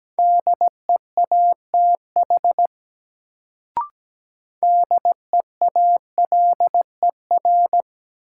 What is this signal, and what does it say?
Morse code, heard as a keyed tone, about 17 wpm, 700 Hz. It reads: DEATH DEALER
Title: Sentences from Top 1100 Words - New Word Drill Alt - 17wpm